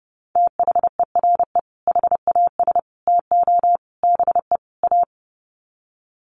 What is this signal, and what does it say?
Morse code, heard as a keyed tone, about 30 wpm, 700 Hz. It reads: THERE HAS TO BE A
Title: Sets of 5 Words Encore 30wpm